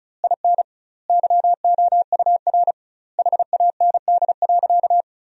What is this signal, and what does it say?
Morse code, heard as a keyed tone, about 36 wpm, 700 Hz. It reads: IN YOUR HAND.
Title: 2x Word-Spacing - Sentences from Top 400 Words - 35wpm